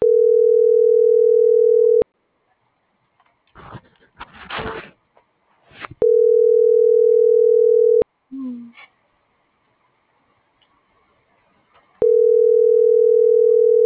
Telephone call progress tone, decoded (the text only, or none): ringback tone